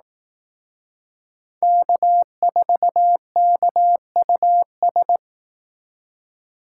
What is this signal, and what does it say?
Morse code, heard as a keyed tone, about 18 wpm, 700 Hz. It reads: K4KUS
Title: Speed Racing - USA 1x3 Call Signs 15wpm